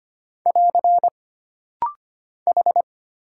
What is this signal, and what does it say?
Morse code, heard as a keyed tone, about 25 wpm, 700 Hz. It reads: <AR> H
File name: Single Character (Letter:Number:Punctuation:Prosign) - Rapid-Fire - 25wpm